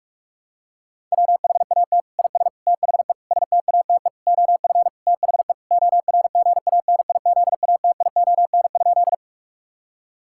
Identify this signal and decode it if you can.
Morse code, heard as a keyed tone, about 46 wpm, 700 Hz. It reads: WHAT IS THE STATE OF THE ORGANIZATION?
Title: Sentences from Top 1000 Words - Review Part 5 - 45wpm